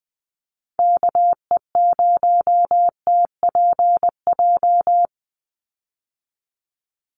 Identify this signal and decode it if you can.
Morse code, heard as a keyed tone, about 20 wpm, 700 Hz. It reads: KE0TPJ
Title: USA Call Signs 20wpm